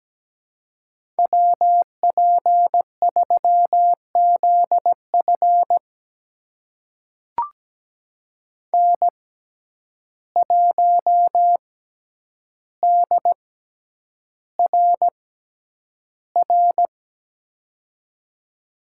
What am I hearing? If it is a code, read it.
Morse code, heard as a keyed tone, about 17 wpm, spaced as for 9 wpm, 700 Hz. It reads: WP3ZF N1DRR
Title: USA Call Signs - 5 Characters - 6x Character Spacing - 17wpm